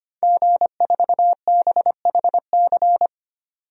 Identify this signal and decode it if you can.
Morse code, heard as a keyed tone, about 25 wpm, 700 Hz. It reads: G4BHC
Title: DXCC-100 Call Signs - 5 Characters - 25wpm